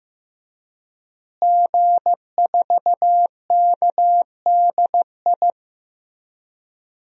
Morse code, heard as a keyed tone, about 15 wpm, 700 Hz. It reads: G4KDI
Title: DXCC-100 Call Signs - 5 Characters - Repeated 3x - 15wpm